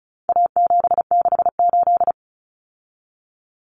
Morse code, heard as a keyed tone, about 34 wpm, 700 Hz. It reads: A768
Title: Common Cut Numbers - 4-Digit 35wpm